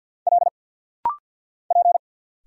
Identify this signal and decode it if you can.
Morse code, heard as a keyed tone, about 50 wpm, 700 Hz. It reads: L P
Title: Single Letters - Rapid-Fire 50wpm